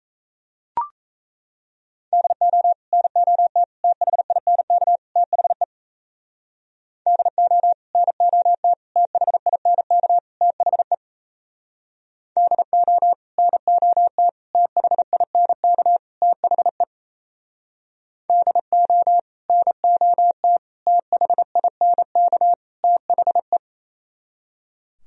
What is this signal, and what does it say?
Morse code, heard as a keyed tone, about 36 wpm, 700 Hz. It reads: DO NOT THINK THE DO NOT THINK THE DO NOT THINK THE DO NOT THINK THE